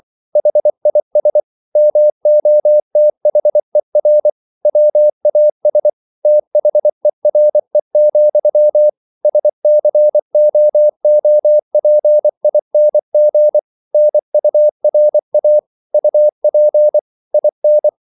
Morse code, heard as a keyed tone, about 24 wpm, 600 Hz. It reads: HIS MOTHER WAS THERE, SCOOPING NURA UP IN